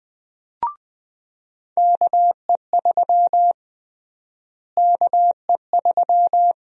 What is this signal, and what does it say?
Morse code, heard as a keyed tone, about 20 wpm, 700 Hz. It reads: KE3 KE3
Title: USA Call Sign Prefixes - Repeated 2x - 20wpm